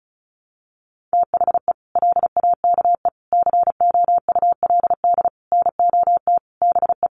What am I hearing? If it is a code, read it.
Morse code, heard as a keyed tone, about 34 wpm, 700 Hz. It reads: THE LAKE COULD NOT BE
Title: Sentences from Top 1100 Words 35wpm